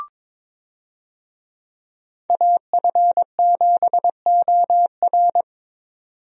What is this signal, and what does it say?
Morse code, heard as a keyed tone, about 22 wpm, 700 Hz. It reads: AF7OR
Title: USA Call Signs - 5 Characters - 22wpm